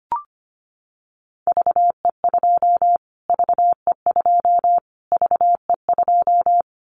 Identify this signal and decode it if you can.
Morse code, heard as a keyed tone, about 25 wpm, 700 Hz. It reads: VE2 VE2 VE2
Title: Callsign Prefixes for Top 25 DXCC - Repeated 3x - 25wpm